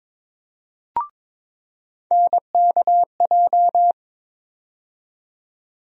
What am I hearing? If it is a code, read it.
Morse code, heard as a keyed tone, about 22 wpm, 700 Hz. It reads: NKJ
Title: USA Call Sign Postfixes - 3 Letters 22wpm